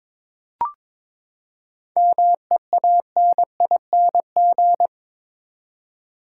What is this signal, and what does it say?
Morse code, heard as a keyed tone, about 22 wpm, 700 Hz. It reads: MEANING
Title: Seven Letter Words - 22wpm